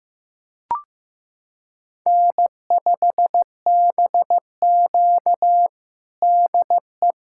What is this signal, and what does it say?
Morse code, heard as a keyed tone, about 15 wpm, 700 Hz. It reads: N5BQ DE
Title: USA Callsign DE Callsign - 4 Characters - 15wpm